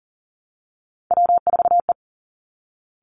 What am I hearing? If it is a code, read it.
Morse code, heard as a keyed tone, about 40 wpm, 700 Hz. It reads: W4E
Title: Three Challenging Letter-Number Combinations 40wpm